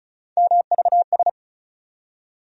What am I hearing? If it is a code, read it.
Morse code, heard as a keyed tone, about 36 wpm, 700 Hz. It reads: MVS
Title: USA Call Sign Postfixes - 3 Letters 35wpm